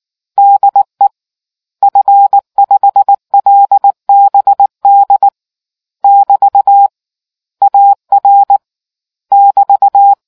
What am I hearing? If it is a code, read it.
Morse code, heard as a keyed tone, about 19 wpm, 800 Hz. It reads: DE F5LBD <BT> AR <BT>